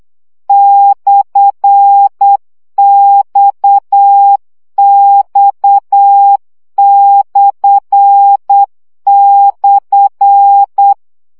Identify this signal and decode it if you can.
Morse code, heard as a keyed tone, about 8 wpm, 800 Hz. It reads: /XX//
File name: CW gyakorlas x per 7 baud